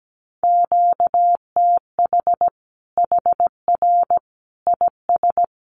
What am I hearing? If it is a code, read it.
Morse code, heard as a keyed tone, about 17 wpm, 700 Hz. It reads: QTH HR IS
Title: Common QSO Phrases 17wpm